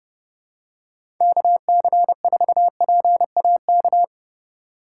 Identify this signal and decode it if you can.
Morse code, heard as a keyed tone, about 31 wpm, 700 Hz. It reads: KC4PAK
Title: USA Call Signs - 6 Characters - 2x Character Spacing - 30wpm